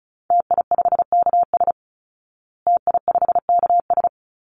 Morse code, heard as a keyed tone, about 34 wpm, 700 Hz. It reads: TI5KS TI5KS
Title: Call Signs Repeated 3x - 35wpm